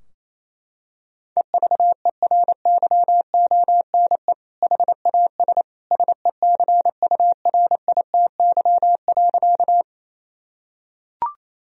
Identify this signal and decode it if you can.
Morse code, heard as a keyed tone, about 28 wpm, 700 Hz. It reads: EVERYONE HAS SECURITY.
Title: Sentences from Top 1000 Words - Review Part 4 - 28wpm